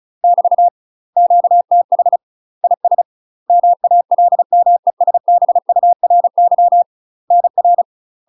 Morse code, heard as a keyed tone, about 36 wpm, 700 Hz. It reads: <BT> QTH IS MALMESBURY NR
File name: UK MASTER QSOs TRAINING / 35WPM / UK MASTER QSOs 0012